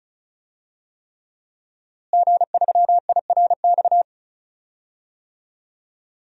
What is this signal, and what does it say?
Morse code, heard as a keyed tone, about 35 wpm, 700 Hz. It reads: G3IRX
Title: DXCC-100 Call Signs - 5 Characters - Repeated 3x - 35wpm